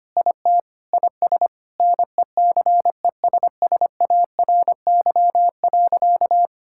Morse code, heard as a keyed tone, about 25 wpm, 700 Hz. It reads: IT IS NECESSARY.
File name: practice / 25wpm / Sentences with 9-Word Phrases - 25wpm